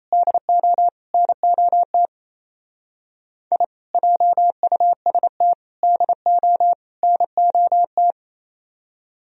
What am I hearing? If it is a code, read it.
Morse code, heard as a keyed tone, about 36 wpm, 700 Hz. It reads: DO NOT I JUST DO NOT